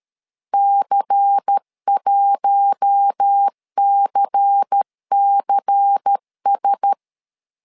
Morse code, heard as a keyed tone, about 13 wpm, 800 Hz. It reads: C1CCS
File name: LEZIONI / 5 LBVC / 5 LBVC 12wpm 1955